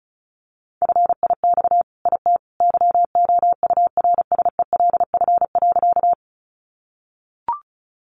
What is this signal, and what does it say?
Morse code, heard as a keyed tone, about 34 wpm, 700 Hz. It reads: FIX IT YOURSELF.
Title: Sentences from Top 1400 Words 35wpm